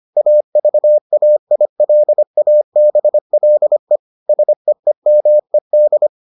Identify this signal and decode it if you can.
Morse code, heard as a keyed tone, about 25 wpm, 600 Hz. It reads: AVAILABLE SEEMED